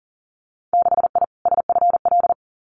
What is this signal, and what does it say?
Morse code, heard as a keyed tone, about 38 wpm, 700 Hz. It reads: 6I SFL
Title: Winter Field Day Exchanges Encore 40wpm